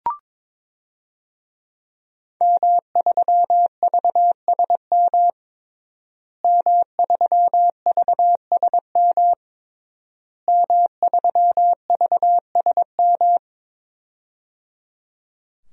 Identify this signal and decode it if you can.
Morse code, heard as a keyed tone, about 22 wpm, 700 Hz. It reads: M3VSM M3VSM M3VSM
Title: DXCC-100 Call Signs - 5 Characters - Repeated 3x - 22wpm